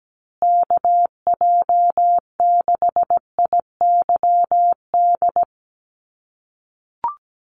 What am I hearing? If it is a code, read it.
Morse code, heard as a keyed tone, about 17 wpm, 700 Hz. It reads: KJ6IYD
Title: USA Call Signs Repeated 2x 17wpm